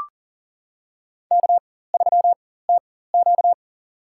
Morse code, heard as a keyed tone, about 40 wpm, spaced as for 19 wpm, 700 Hz. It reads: K3TQ
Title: USA Call Signs - 4 Characters - 4x Character Spacing - 40wpm